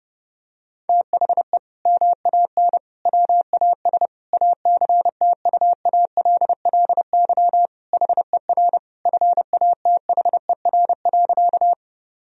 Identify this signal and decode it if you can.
Morse code, heard as a keyed tone, about 30 wpm, 700 Hz. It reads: THE MAN WAS ACTUALLY HER FATHER.
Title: Sentences from Top 1000 Words - Review Part 4 - 30wpm